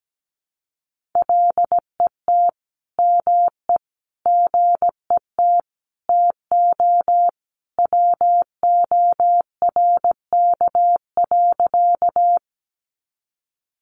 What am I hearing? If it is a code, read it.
Morse code, heard as a keyed tone, about 17 wpm, 700 Hz. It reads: LET ME GET TO WORK.
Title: Sentences from Top 300 Words 17wpm